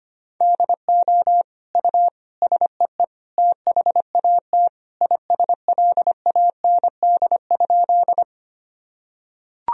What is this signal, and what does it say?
Morse code, heard as a keyed tone, about 25 wpm, 700 Hz. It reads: DO U SEE THAT ISLAND?
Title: Sentences from Top 500 Words and Abbreviations 25wpm